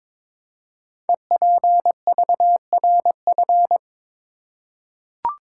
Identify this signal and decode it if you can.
Morse code, heard as a keyed tone, about 22 wpm, 700 Hz. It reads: EPVRF
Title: Code Groups - 5 Letters - 22wpm